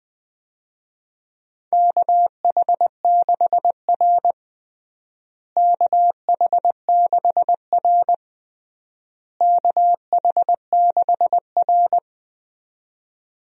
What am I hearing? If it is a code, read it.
Morse code, heard as a keyed tone, about 20 wpm, 700 Hz. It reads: KH6R KH6R KH6R
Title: USA Call Signs - 4 Characters - Repeated 3x - 20wpm